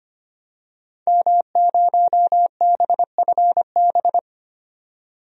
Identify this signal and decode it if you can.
Morse code, heard as a keyed tone, about 25 wpm, 700 Hz. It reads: M0BFB
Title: DXCC-100 Call Signs - 5 Characters - 25wpm